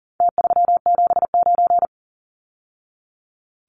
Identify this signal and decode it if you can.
Morse code, heard as a keyed tone, about 40 wpm, 700 Hz. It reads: T379
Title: Numbers - Contest Style - 4-Digit 40wpm